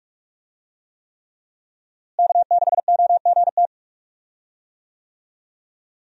Morse code, heard as a keyed tone, about 46 wpm, 700 Hz. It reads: K6OGT
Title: USA Call Signs - 5 Characters - Repeated 3x - 45wpm